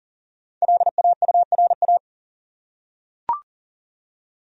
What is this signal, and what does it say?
Morse code, heard as a keyed tone, about 40 wpm, 700 Hz. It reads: LAURA
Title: USA Common Names - 40wpm